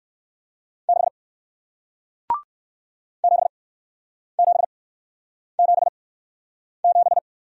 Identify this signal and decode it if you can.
Morse code, heard as a keyed tone, about 61 wpm, spaced as for 9 wpm, 700 Hz. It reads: 6 7777